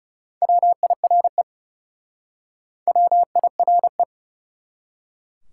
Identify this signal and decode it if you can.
Morse code, heard as a keyed tone, about 37 wpm, 700 Hz. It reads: WIRE WIRE
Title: Speed-Racing Top 180 QSO Elements 30wpm